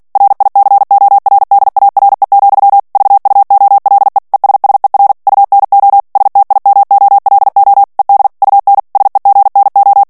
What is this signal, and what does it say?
Morse code, heard as a keyed tone, about 48 wpm, 800 Hz. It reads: RICORDARE, VUOLE ESSER UNO STIMOLO ED UN SEGNO
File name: morse mp3 / ITALIEN texte clair 1 vitesse 45 mots minute